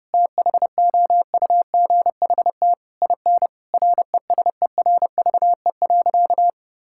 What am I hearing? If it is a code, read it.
Morse code, heard as a keyed tone, about 32 wpm, 700 Hz. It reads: THOUGHT IN RESERVE.
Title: Sentences from Top 1000 Words - Review Part 6 - 30wpm